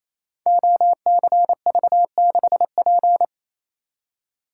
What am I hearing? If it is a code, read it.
Morse code, heard as a keyed tone, about 28 wpm, 690 Hz. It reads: OCV6P
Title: Code Groups - 5 Letter:Numbers - 28wpm